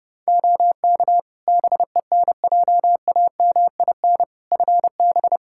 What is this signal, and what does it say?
Morse code, heard as a keyed tone, about 32 wpm, 700 Hz. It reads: OK BENJAMIN FB
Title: K1USN SST QSO Practice 30wpm